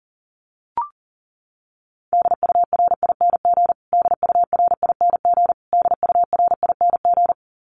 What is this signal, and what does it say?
Morse code, heard as a keyed tone, about 40 wpm, 700 Hz. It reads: DURING DURING DURING
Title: Top 500 Words Repeated 40wpm